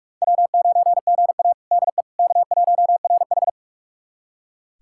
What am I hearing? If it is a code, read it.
Morse code, heard as a keyed tone, about 42 wpm, 700 Hz. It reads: W9GA DE K1RH
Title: USA Callsign DE Callsign - 4 Characters - 45wpm